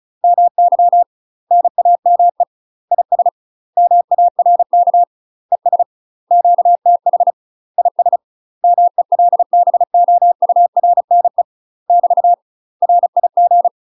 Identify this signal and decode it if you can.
Morse code, heard as a keyed tone, about 34 wpm, 700 Hz. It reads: MY NAME IS MARK ES QTH IS MELBOURNE <BT> RIG
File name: UK MASTER QSOs TRAINING / 35WPM / UK MASTER QSOs 0025